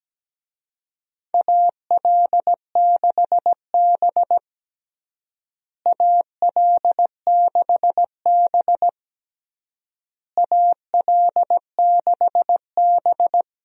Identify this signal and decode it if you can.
Morse code, heard as a keyed tone, about 17 wpm, 700 Hz. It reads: AL6B AL6B AL6B
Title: USA Call Signs - 4 Characters - Repeated 3x - 17wpm